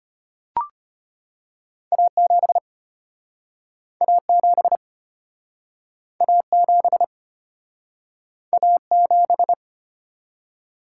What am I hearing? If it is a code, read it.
Morse code, heard as a keyed tone, about 34 wpm, 700 Hz. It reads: A7 A7 A7 A7